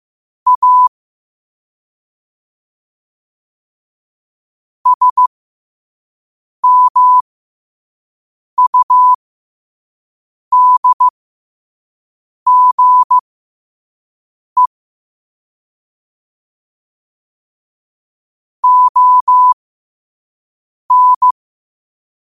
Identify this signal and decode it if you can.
Morse code, heard as a keyed tone, about 15 wpm, spaced as for 5 wpm, 1000 Hz. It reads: A SMUDGE ON
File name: qotdfiles / 5wpm